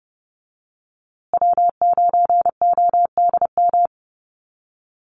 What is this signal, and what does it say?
Morse code, heard as a keyed tone, about 30 wpm, 700 Hz. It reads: W9ODM
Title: USA Call Signs Repeated 3x 30wpm